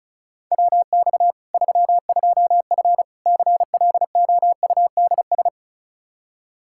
Morse code, heard as a keyed tone, about 36 wpm, 700 Hz. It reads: WX 32F CLOUDS
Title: Practice QSO - Exchange 2 - Info - 35wpm